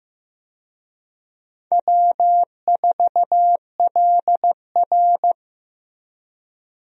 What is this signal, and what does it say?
Morse code, heard as a keyed tone, about 15 wpm, 700 Hz. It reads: W4LR
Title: USA Call Signs - 4 Characters - Repeated 2x - 15wpm